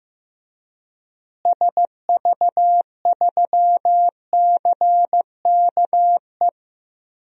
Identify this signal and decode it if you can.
Morse code, heard as a keyed tone, about 15 wpm, 700 Hz. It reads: SV3CKE
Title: DXCC-100 Call Signs - 6 Characters - Repeated 3x - 15wpm